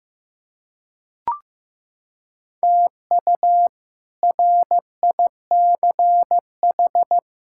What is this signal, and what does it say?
Morse code, heard as a keyed tone, about 15 wpm, 700 Hz. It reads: TU RICH
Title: K1USN SST QSO Practice - 15wpm